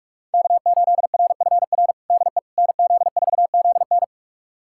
Morse code, heard as a keyed tone, about 45 wpm, 700 Hz. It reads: K8RFR DE NZ4ZN
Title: USA Callsign DE Callsign - 5 Characters - 45wpm